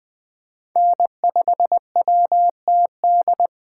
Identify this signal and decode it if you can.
Morse code, heard as a keyed tone, about 20 wpm, 700 Hz. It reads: N5WTD
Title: Speed Racing - USA 1x3 Call Signs 17wpm